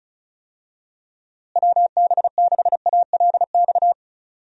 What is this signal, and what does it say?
Morse code, heard as a keyed tone, about 36 wpm, 700 Hz. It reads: WB6ALX
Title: USA Call Signs - 6 Characters - Repeated 3x - 35wpm